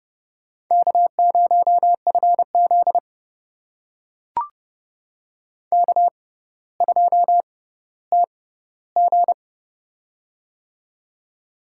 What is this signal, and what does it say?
Morse code, heard as a keyed tone, about 30 wpm, spaced as for 15 wpm, 700 Hz. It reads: K0FZ K2TG
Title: USA Call Signs - 4 Characters - 6x Character Spacing - 30wpm